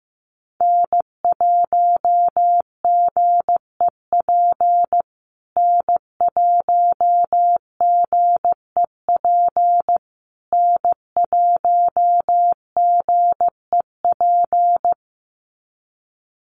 Morse code, heard as a keyed tone, about 15 wpm, 700 Hz. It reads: N1GEP N1GEP N1GEP
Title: USA Call Signs Repeated 3x 15wpm